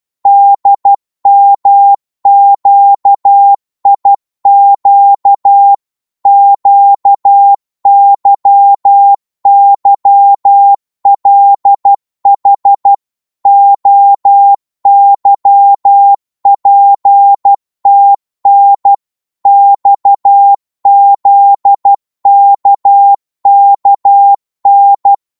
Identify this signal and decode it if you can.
Morse code, heard as a keyed tone, about 12 wpm, 800 Hz. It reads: DMQIQ QYYLH OYPTN XZKKN